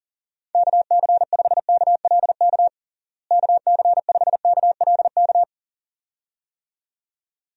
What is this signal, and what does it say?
Morse code, heard as a keyed tone, about 37 wpm, 700 Hz. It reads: KC5KLK KC5KLK
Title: USA Call Signs - 6 Characters - Repeated 2x - 40wpm